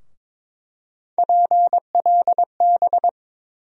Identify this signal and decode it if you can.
Morse code, heard as a keyed tone, about 22 wpm, 700 Hz. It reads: PLB
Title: USA Call Sign Postfixes - 3 Letters 22wpm